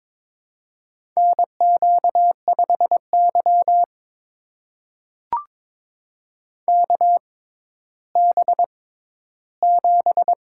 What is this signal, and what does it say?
Morse code, heard as a keyed tone, about 22 wpm, spaced as for 13 wpm, 700 Hz. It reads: NQ5Y KB7
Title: USA Call Signs - 4 Characters - 6x Character Spacing - 22wpm